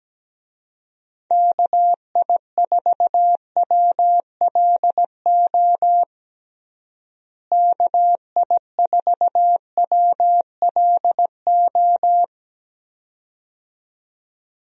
Morse code, heard as a keyed tone, about 17 wpm, 700 Hz. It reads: KI4WLO KI4WLO
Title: USA Call Signs - 6 Characters - Repeated 2x - 17wpm